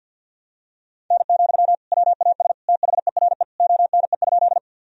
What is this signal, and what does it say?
Morse code, heard as a keyed tone, about 47 wpm, 700 Hz. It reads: N, WAS THERE ONE?
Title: Sentences from Top 100 Words and Abbreviations 50wpm